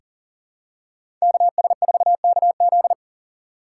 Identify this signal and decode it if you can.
Morse code, heard as a keyed tone, about 37 wpm, 700 Hz. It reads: KS4KZ
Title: USA Call Signs - 6x Character Spacing - 40wpm